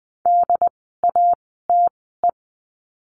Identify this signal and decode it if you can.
Morse code, heard as a keyed tone, about 20 wpm, spaced as for 14 wpm, 700 Hz. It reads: DATE
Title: 068 Lesson 12f - Letter d - Words 20wpm